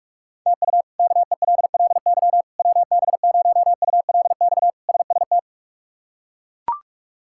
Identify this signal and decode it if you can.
Morse code, heard as a keyed tone, about 42 wpm, 700 Hz. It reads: TU KELLY WB0ULX SST
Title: K1USN SST QSO Practice - 45wpm